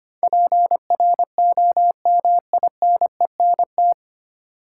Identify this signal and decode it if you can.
Morse code, heard as a keyed tone, about 25 wpm, 700 Hz. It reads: PROMINENT
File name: practice / 25wpm / 9 Letter Words - First 3 Letters Given - 25wpm